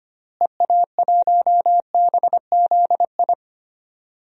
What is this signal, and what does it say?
Morse code, heard as a keyed tone, about 25 wpm, 700 Hz. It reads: EA1BZI
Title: DXCC-100 Call Signs - 6 Characters - Repeated 3x - 25wpm